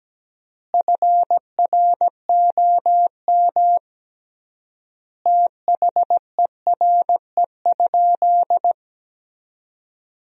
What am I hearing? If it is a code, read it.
Morse code, heard as a keyed tone, about 17 wpm, 700 Hz. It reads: FROM THERE?
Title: 3x Word-Spacing - Sentences from Top 100 Words - 17wpm